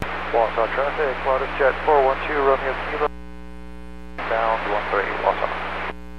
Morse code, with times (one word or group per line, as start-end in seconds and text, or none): none